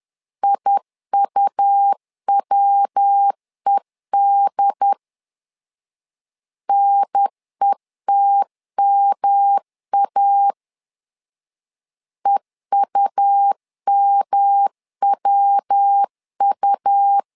Morse code, heard as a keyed tone, about 10 wpm, 800 Hz. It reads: IUWED NETMA EUMWU
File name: LEZIONI / 3 MWRO / 3 MW 10wpm 1959